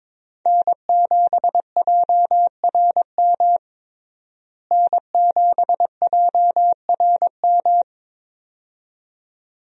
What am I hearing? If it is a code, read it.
Morse code, heard as a keyed tone, about 22 wpm, 700 Hz. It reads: N7JRM N7JRM
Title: USA Call Signs - 5 Characters - Repeated 2x - 22wpm